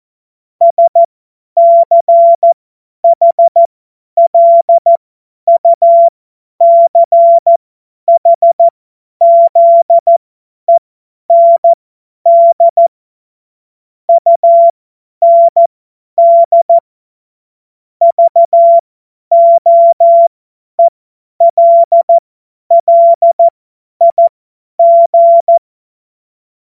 Morse code, heard as a keyed tone, about 14 wpm, spaced as for 10 wpm, 680 Hz. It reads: SCHLUCHZEND UND VOELLIG
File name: Morsekurs / Witze / Witz 08. Februar 2022 50 F70